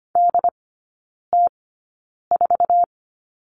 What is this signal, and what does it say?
Morse code, heard as a keyed tone, about 25 wpm, spaced as for 9 wpm, 700 Hz. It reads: DT4
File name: Lesson 15e - Number 5 - Three Characters 25wpm